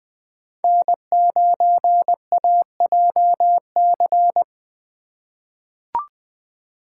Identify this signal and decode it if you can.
Morse code, heard as a keyed tone, about 20 wpm, 700 Hz. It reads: N9AJC
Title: USA Call Signs - 2x Character Spacing - 20wpm